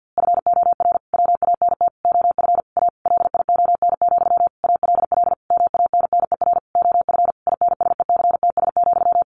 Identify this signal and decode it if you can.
Morse code, heard as a keyed tone, about 50 wpm, 700 Hz. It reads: FOR WANT OF A LION, ALL MANNER OF INSECTS,
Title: A Sound Of Thunder 50wpm